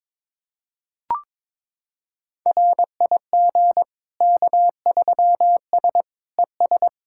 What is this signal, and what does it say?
Morse code, heard as a keyed tone, about 22 wpm, 700 Hz. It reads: RIG K3S ES
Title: K1USN SST QSO Practice 22wpm